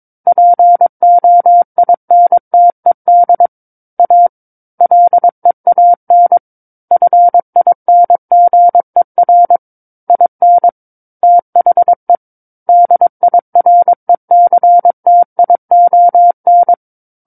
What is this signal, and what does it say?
Morse code, heard as a keyed tone, about 22 wpm, 700 Hz. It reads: POINTED A LEAN FINGER IN THE DIRECTION
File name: Book One / 13-Chapter 13 - 22 WPM